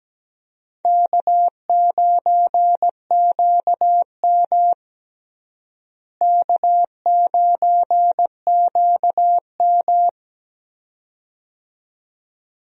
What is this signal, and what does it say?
Morse code, heard as a keyed tone, about 17 wpm, 700 Hz. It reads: K9QM K9QM
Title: USA Call Signs - 4 Characters - Repeated 2x - 17wpm